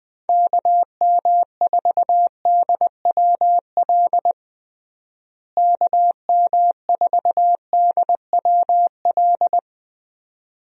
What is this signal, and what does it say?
Morse code, heard as a keyed tone, about 21 wpm, 700 Hz. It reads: KM4DWL KM4DWL
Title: USA Call Signs - 6 Characters - Repeated 2x - 20wpm